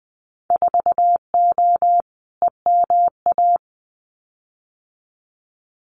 Morse code, heard as a keyed tone, about 20 wpm, 700 Hz. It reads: 4O EMA
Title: Winter Field Day Exchanges 20wpm